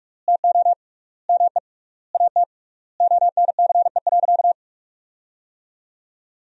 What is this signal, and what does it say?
Morse code, heard as a keyed tone, about 41 wpm, 700 Hz. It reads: TO ME AT ONCE.
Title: 3x Word-Spacing - Sentences from Top 500 Words - 45wpm